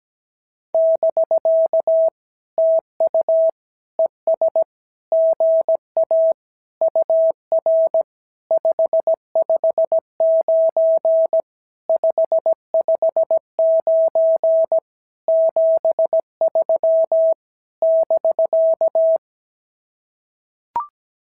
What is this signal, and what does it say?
Morse code, heard as a keyed tone, about 17 wpm, 650 Hz. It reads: <BK> TU ES GA UR 559 559 73 <BK>
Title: SOTA QSO Practice 17wpm